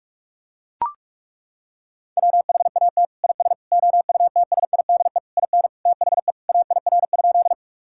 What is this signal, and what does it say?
Morse code, heard as a keyed tone, about 41 wpm, 700 Hz. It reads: WHAT IS OUTSIDE IN THE AIR?
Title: Sentences from Top 1000 Words - Review Part 6 - 45wpm